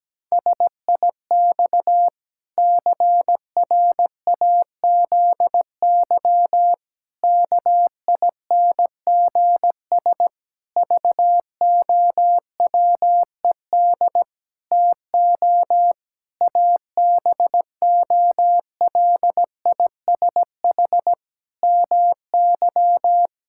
Morse code, heard as a keyed tone, about 17 wpm, 700 Hz. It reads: SIX CRAZY KINGS VOWED TO ABOLISH MY
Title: Pangram Challenge v2 - 17wpm